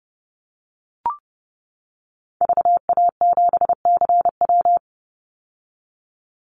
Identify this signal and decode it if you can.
Morse code, heard as a keyed tone, about 31 wpm, 700 Hz. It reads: VA7CW
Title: Call Signs - 30wpm